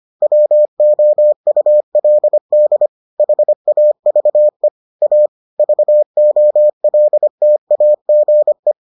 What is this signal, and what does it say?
Morse code, heard as a keyed tone, about 25 wpm, 600 Hz. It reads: WOULD HAVE A VOLTAGE